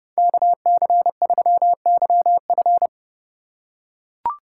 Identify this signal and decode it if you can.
Morse code, heard as a keyed tone, about 31 wpm, 700 Hz. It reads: KC3YF
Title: USA Call Signs - 5 Characters - 2x Character Spacing - 30wpm